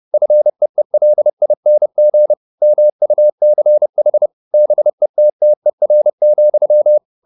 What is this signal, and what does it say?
Morse code, heard as a keyed tone, about 30 wpm, 600 Hz. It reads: FEELING MUCH BETTER,
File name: the old man and the sea / the old man and the sea 1 30wpm